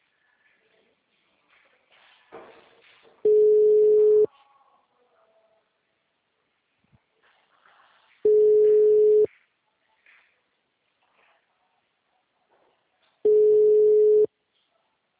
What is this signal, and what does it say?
Ringback tone (Europe, 425 Hz).
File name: out- -2015- - - .201